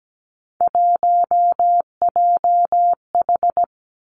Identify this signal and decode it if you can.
Morse code, heard as a keyed tone, about 17 wpm, 700 Hz. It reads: 1JH
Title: Three Challenging Letter-Number Combinations 17wpm